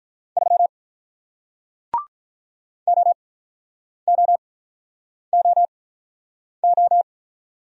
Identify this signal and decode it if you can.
Morse code, heard as a keyed tone, about 47 wpm, spaced as for 9 wpm, 700 Hz. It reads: 3 OOOO